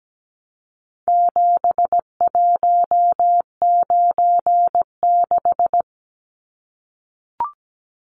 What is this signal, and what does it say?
Morse code, heard as a keyed tone, about 17 wpm, 700 Hz. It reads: 7196
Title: Numbers - 4-Digit 17wpm